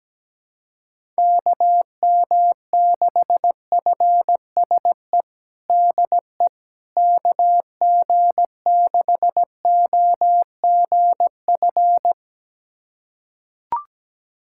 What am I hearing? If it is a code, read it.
Morse code, heard as a keyed tone, about 17 wpm, 700 Hz. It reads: KM6FSE DE KG6OGF
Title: USA Callsign DE Callsign - 6 Characters - 17wpm